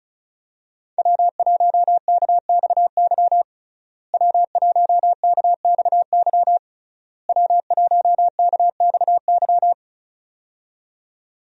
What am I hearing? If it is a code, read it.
Morse code, heard as a keyed tone, about 36 wpm, 700 Hz. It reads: W1KXY W1KXY W1KXY
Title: USA Call Signs - 5 Characters - Repeated 3x - 35wpm